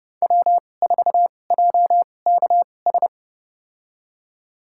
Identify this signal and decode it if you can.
Morse code, heard as a keyed tone, about 30 wpm, spaced as for 22 wpm, 700 Hz. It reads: W4JKS
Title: USA Call Signs - 5 Characters - 2x Character Spacing - 30wpm